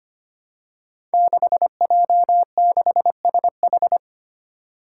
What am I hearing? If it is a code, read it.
Morse code, heard as a keyed tone, about 25 wpm, 700 Hz. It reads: 6J6SH
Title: Five Challenging Letter-Number Combinations 25wpm